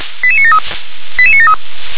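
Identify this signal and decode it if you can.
Five-tone selective call (ZVEI), heard as quick runs of five tones, tones of about 80 ms. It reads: 88973 88973
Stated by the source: ZVEI five-tone selective call 88460 received by SDR.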